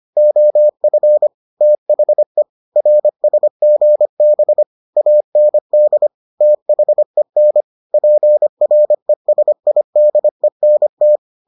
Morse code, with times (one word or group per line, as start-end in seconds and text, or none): 0.2-1.3 OF
1.6-2.4 THE
2.8-4.6 RSGB
5.0-6.1 AND
6.4-7.6 THEN
7.9-11.2 PRESIDENT